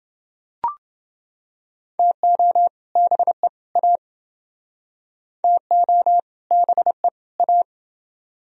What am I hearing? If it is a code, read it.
Morse code, heard as a keyed tone, about 29 wpm, 700 Hz. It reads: TO BE A TO BE A